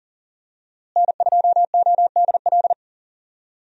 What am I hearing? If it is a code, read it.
Morse code, heard as a keyed tone, about 39 wpm, 700 Hz. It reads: N2ODL
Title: USA Call Signs - 5 Characters - Repeated 2x - 40wpm